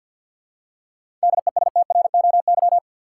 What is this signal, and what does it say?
Morse code, heard as a keyed tone, about 53 wpm, 700 Hz. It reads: DESTROY
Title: Seven Letter Words - 50wpm